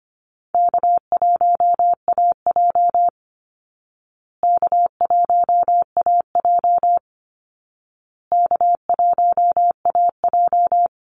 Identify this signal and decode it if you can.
Morse code, heard as a keyed tone, about 25 wpm, 700 Hz. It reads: K1AJ K1AJ K1AJ
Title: Call Signs Repeated 2x - 25wpm